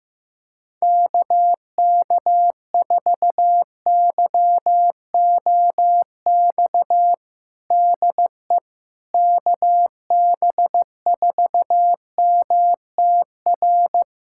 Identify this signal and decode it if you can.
Morse code, heard as a keyed tone, about 16 wpm, 700 Hz. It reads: KK4YOX DE KB4MTR
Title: USA Callsign DE Callsign 15wpm